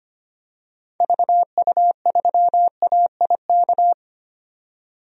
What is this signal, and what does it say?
Morse code, heard as a keyed tone, about 25 wpm, 690 Hz. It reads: VU3AIK
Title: DXCC-100 Call Signs - 6 Characters - 25wpm